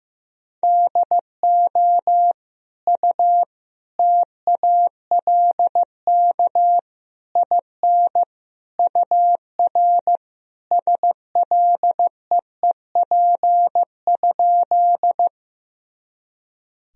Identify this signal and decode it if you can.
Morse code, heard as a keyed tone, about 15 wpm, 700 Hz. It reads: DO U TALK IN UR SLEEP?
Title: Sentences from Top 500 Words and Abbreviations 15wpm